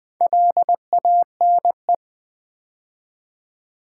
Morse code, heard as a keyed tone, about 20 wpm, 700 Hz. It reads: LANE
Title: Build A Word - 2 Letters At A Time - 4-Letter Words - 20wpm